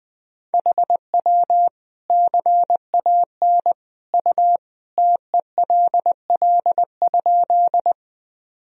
Morse code, heard as a keyed tone, about 20 wpm, 700 Hz. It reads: HW CAN U TELL?
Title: Sentences from Top 200 Words and Abbreviations 20wpm